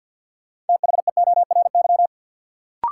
Morse code, heard as a keyed tone, about 52 wpm, 700 Hz. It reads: THEORY
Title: Sentences from Top 600 Words - New Word Drill - 50wpm